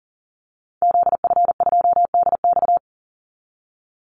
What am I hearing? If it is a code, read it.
Morse code, heard as a keyed tone, about 40 wpm, 700 Hz. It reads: ZF2DX
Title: Call Signs - 40wpm